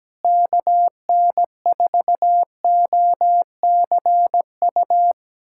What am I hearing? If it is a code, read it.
Morse code, heard as a keyed tone, about 17 wpm, 700 Hz. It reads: KN4OCU
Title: USA Callsign DE Callsign - 6 Characters - 17wpm